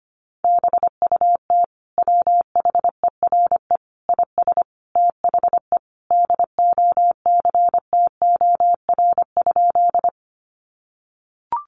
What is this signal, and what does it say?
Morse code, heard as a keyed tone, about 25 wpm, 700 Hz. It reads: BUT WHERE IS THE DOCTOR?
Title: Sentences from Top 1500 Words 25wpm